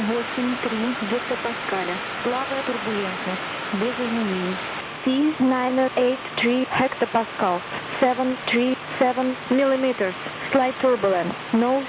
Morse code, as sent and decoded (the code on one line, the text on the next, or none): none